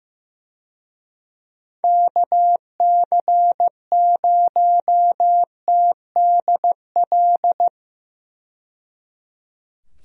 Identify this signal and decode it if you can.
Morse code, heard as a keyed tone, about 15 wpm, 700 Hz. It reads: KC0TDL
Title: USA Call Signs - 6 Characters - 15wpm